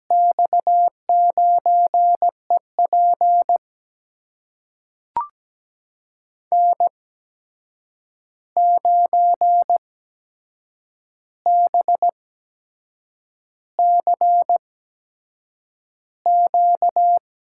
Morse code, heard as a keyed tone, about 17 wpm, spaced as for 7 wpm, 700 Hz. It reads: X9EP N9BCQ
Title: USA Call Signs - 5 Characters - 8x Character Spacing - 17wpm